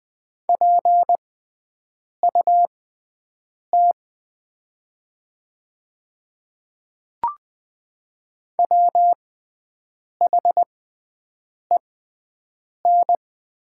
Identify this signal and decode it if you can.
Morse code, heard as a keyed tone, about 20 wpm, spaced as for 7 wpm, 700 Hz. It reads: PUT WHEN
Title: 6x Character Spacing - Top 200 Words - 20wpm